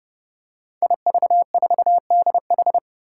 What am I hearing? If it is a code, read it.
Morse code, heard as a keyed tone, about 30 wpm, 700 Hz. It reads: IV4DH
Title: Five Challenging Letter-Number Combinations 30wpm